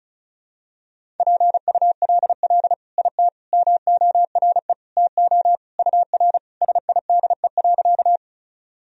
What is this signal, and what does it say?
Morse code, heard as a keyed tone, about 36 wpm, 700 Hz. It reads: PULL IT MORE TO UR SIDE.
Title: Sentences from Top 400 Words and Abbreviations 35wpm